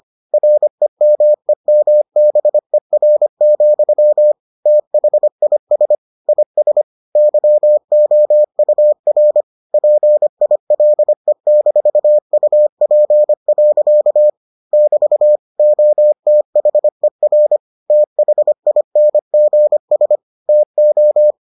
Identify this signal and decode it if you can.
Morse code, heard as a keyed tone, about 25 wpm, 600 Hz. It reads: REMEMBER, THIS IS YOUR PILE-UP. <BT> OTHER THINGS TO